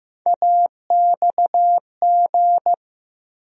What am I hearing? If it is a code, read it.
Morse code, heard as a keyed tone, about 15 wpm, 700 Hz. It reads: AXG
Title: USA Call Sign Postfixes - 3 Letters 15wpm